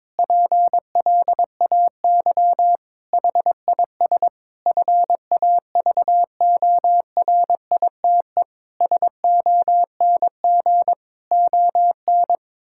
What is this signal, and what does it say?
Morse code, heard as a keyed tone, about 22 wpm, 700 Hz. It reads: PLAY HIS FAVORITE SONG ON
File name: Difficult Sentences with Hints - 22wpm